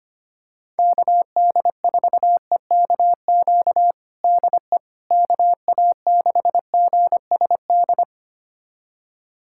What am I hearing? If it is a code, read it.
Morse code, heard as a keyed tone, about 25 wpm, 700 Hz. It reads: KD4EKQ DE KA6GSD
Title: USA Callsign DE Callsign - 6 Characters - 25wpm